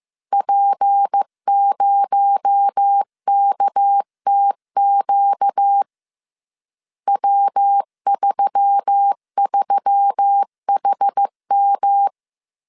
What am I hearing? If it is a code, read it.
Morse code, heard as a keyed tone, about 15 wpm, 800 Hz. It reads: P0KTQ W33HM